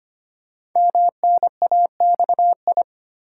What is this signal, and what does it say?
Morse code, heard as a keyed tone, about 25 wpm, 700 Hz. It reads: MNAXI
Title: Code Groups - 5 Letters - 25wpm